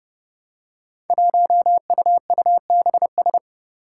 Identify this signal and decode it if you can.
Morse code, heard as a keyed tone, about 29 wpm, 700 Hz. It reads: 1UUBS
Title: Five Challenging Letter-Number Combinations 30wpm